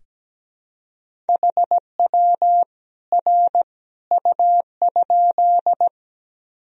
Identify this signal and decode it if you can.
Morse code, heard as a keyed tone, about 17 wpm, 700 Hz. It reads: HW R U?
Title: Sentences from Top 100 Words and Abbreviations 17wpm